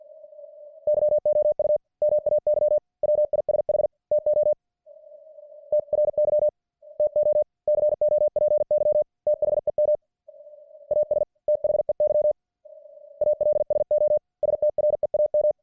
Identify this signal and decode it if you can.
Morse code, heard as a keyed tone, about 50 wpm, 600 Hz. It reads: YOU MAY WISH TO TRY TO COPY THEM AS THEY ALSO STREAM